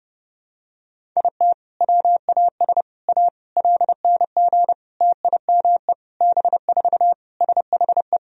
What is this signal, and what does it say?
Morse code, heard as a keyed tone, about 30 wpm, 700 Hz. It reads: IT WAS A LNG TIME B4 SHE
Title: Sentences from Top 500 Words and Abbreviations 30wpm